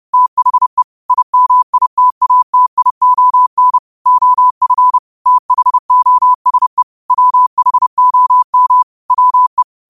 Morse code, heard as a keyed tone, about 30 wpm, 1000 Hz. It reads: THE IMITATION OF THOSE WHOM WE